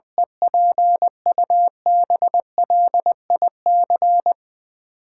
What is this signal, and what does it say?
Morse code, heard as a keyed tone, about 21 wpm, 700 Hz. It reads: EPUBLIC
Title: Sentences from Top 1500 Words - New Word Drill Alt - 20wpm